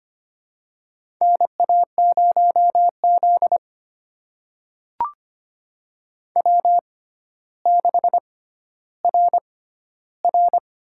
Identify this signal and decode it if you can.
Morse code, heard as a keyed tone, about 25 wpm, spaced as for 13 wpm, 700 Hz. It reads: NA0Z W6RR
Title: USA Call Signs - 4 Characters - 6x Character Spacing - 25wpm